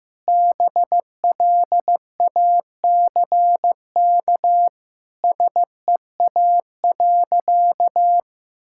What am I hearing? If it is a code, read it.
Morse code, heard as a keyed tone, about 15 wpm, 700 Hz. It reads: BLACK SEA.
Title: Sentences from Top 1000 Words - Review Part 5 - 15wpm